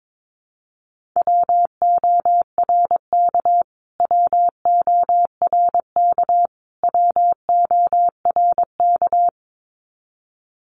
Morse code, heard as a keyed tone, about 22 wpm, 700 Hz. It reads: WORK WORK WORK
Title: Top 200 Words Repeated 22wpm